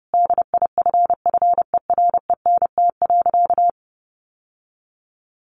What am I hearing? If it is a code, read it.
Morse code, heard as a keyed tone, about 30 wpm, 700 Hz. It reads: DIFFERENT.
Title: Sentences from Top 600 Words 30wpm